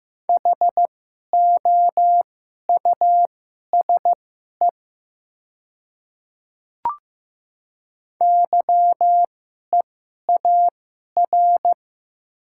Morse code, heard as a keyed tone, about 15 wpm, spaced as for 11 wpm, 700 Hz. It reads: HOUSE YEAR
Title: 2x Character Spacing - Top 500 Words - 15wpm